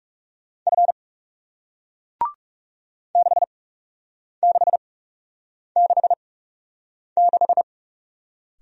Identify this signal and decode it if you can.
Morse code, heard as a keyed tone, about 41 wpm, spaced as for 8 wpm, 700 Hz. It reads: F 6666